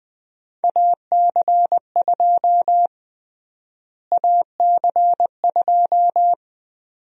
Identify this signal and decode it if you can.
Morse code, heard as a keyed tone, about 20 wpm, 700 Hz. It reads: AC2 AC2
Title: USA Call Sign Prefixes - Repeated 2x - 20wpm